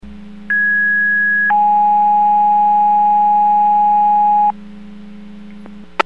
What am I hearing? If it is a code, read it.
Two-tone page, heard as a one-second tone, then a three-second tone, about 1650 then 830 Hz. It is heard from 0.5 to 4.5 s.